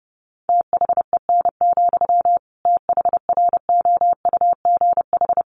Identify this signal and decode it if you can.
Morse code, heard as a keyed tone, about 30 wpm, 700 Hz. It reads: THEN, THROUGH
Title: Sentences from Top 1400 Words 30wpm